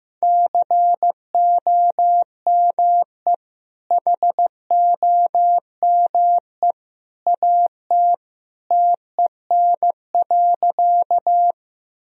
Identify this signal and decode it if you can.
Morse code, heard as a keyed tone, about 15 wpm, 700 Hz. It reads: COME HOME AT TEN.
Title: Sentences from Top 1000 Words - Review Part 2 - 15wpm